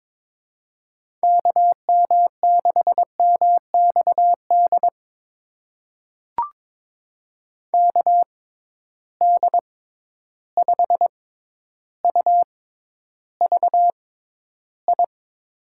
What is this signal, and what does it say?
Morse code, heard as a keyed tone, about 22 wpm, spaced as for 11 wpm, 700 Hz. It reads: KM6MXD KD5UVI
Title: USA Call Signs - 6 Characters - 6x Character Spacing - 22wpm